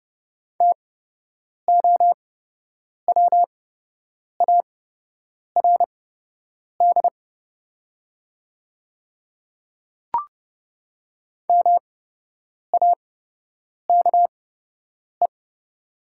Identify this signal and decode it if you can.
Morse code, heard as a keyed tone, about 30 wpm, spaced as for 8 wpm, 700 Hz. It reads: TOWARD MAKE
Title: 8x Character Spacing - Top 500 Words - 30wpm